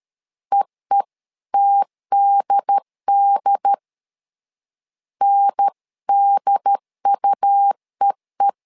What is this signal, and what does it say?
Morse code, heard as a keyed tone, about 13 wpm, 800 Hz. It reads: EETDD NDUEE
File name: LEZIONI / 3 MWRO / 3 M 12wpm 1541